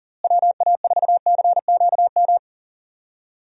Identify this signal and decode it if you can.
Morse code, heard as a keyed tone, about 40 wpm, 700 Hz. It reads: WA4CQM
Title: USA Call Signs - 6 Characters - Repeated 2x - 40wpm